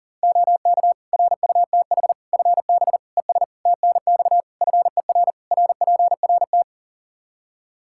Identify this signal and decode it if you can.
Morse code, heard as a keyed tone, about 40 wpm, 700 Hz. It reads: OK RUTH FB ES TNX FER RPRT
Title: Practice QSO - Exchange 2 - Info - 40wpm